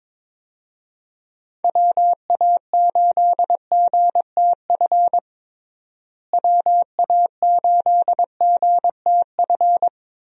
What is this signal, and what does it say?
Morse code, heard as a keyed tone, about 21 wpm, 700 Hz. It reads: WA8GTF WA8GTF
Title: USA Call Signs - 6 Characters - Repeated 3x - 22wpm